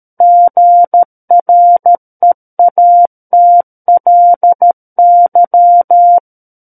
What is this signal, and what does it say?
Morse code, heard as a keyed tone, about 13 wpm, 700 Hz. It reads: GREATLY
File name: Book One / 04-Chapter 4 - 13 WPM